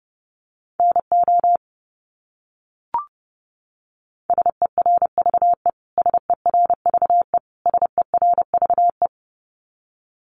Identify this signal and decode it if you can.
Morse code, heard as a keyed tone, about 28 wpm, 700 Hz. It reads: NO SERVE SERVE SERVE
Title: Top 500 Words Repeated 30wpm